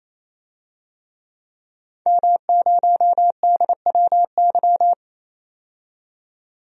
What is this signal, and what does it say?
Morse code, heard as a keyed tone, about 28 wpm, 700 Hz. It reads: M0DWY
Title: DXCC-100 Call Signs - 5 Characters - 28wpm